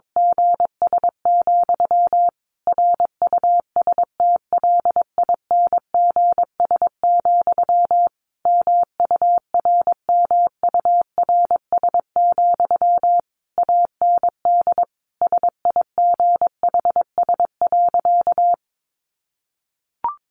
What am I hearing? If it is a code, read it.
Morse code, heard as a keyed tone, about 22 wpm, 700 Hz. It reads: GS, RUSTLINGS, MURMURS, AND SIGHS.
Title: A Sound Of Thunder 22wpm